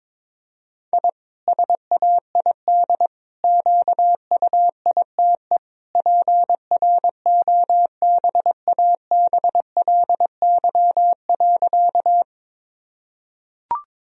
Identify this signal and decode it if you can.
Morse code, heard as a keyed tone, about 22 wpm, 700 Hz. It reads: I SAID QUITE PROBABLY.
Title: Sentences from Top 1000 Words - Review Part 4 - 22wpm